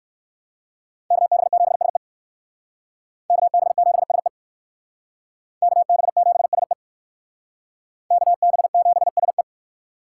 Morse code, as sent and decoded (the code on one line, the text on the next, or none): -.- -... --... ... / -.- -... --... ... / -.- -... --... ... / -.- -... --... ... .
KB7SE KB7SE KB7SE KB7SE